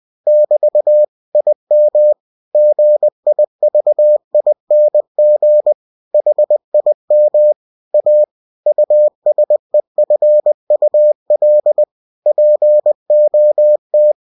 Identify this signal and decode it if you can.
Morse code, heard as a keyed tone, about 20 wpm, 600 Hz. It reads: <BT> IM GIVING HIM A USEFUL POT
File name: cw-books / winnie-the-pooh / winnie-20wpm-page-0051